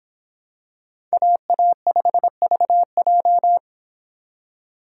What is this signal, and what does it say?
Morse code, heard as a keyed tone, about 26 wpm, 700 Hz. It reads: AA5VJ